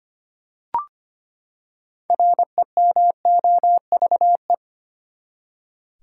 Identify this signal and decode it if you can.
Morse code, heard as a keyed tone, about 25 wpm, 700 Hz. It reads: REMOVE
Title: Six Letter Words - 25wpm